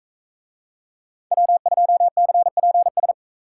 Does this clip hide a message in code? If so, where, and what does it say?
Yes: Morse code from 1.3 to 3.1 s, heard as a keyed tone, about 40 wpm, 700 Hz. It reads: W2CPS